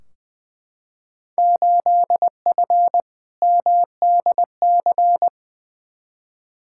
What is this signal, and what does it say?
Morse code, heard as a keyed tone, about 20 wpm, 700 Hz. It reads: 8F MDC
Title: Field Day Exchanges Encore - 20wpm